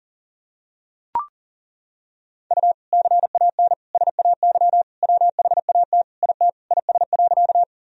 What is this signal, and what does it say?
Morse code, heard as a keyed tone, about 40 wpm, 700 Hz. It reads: U CAN SAY WHAT IT IS.
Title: Sentences from Top 200 Words and Abbreviations 40wpm